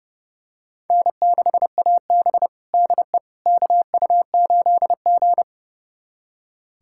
Morse code, heard as a keyed tone, about 32 wpm, 700 Hz. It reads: N6AB DE KU8G
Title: USA Callsign DE Callsign - 4 Characters - 30wpm